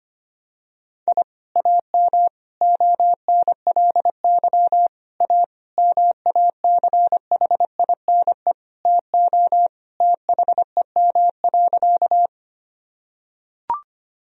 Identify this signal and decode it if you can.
Morse code, heard as a keyed tone, about 25 wpm, 700 Hz. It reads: I AM ONLY A MACHINE TO THEM.
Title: Sentences from Top 1000 Words - Review Part 2 - 25wpm